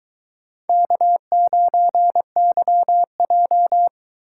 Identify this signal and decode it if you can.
Morse code, heard as a keyed tone, about 23 wpm, 700 Hz. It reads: K9YJ